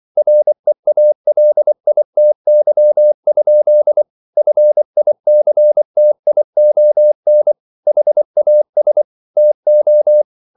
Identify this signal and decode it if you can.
Morse code, heard as a keyed tone, about 24 wpm, 600 Hz. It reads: REALITY? FICTION HAS TO
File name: CW / Practice / EnglishQuotations0424